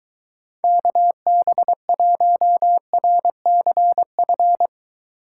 Morse code, heard as a keyed tone, about 22 wpm, 700 Hz. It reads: KB1RCF